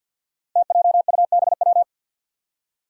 Morse code, heard as a keyed tone, about 51 wpm, 700 Hz. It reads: TJUBW
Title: Five Challenging Letter-Number Combinations 50wpm